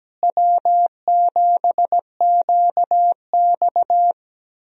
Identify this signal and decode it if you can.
Morse code, heard as a keyed tone, about 17 wpm, 700 Hz. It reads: W7QX
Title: USA Call Signs - 4 Characters - Repeated 2x - 17wpm